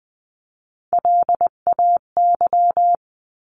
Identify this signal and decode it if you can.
Morse code, heard as a keyed tone, about 20 wpm, 700 Hz. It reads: LAY
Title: 259 Lesson 42d - Words 18-20wpm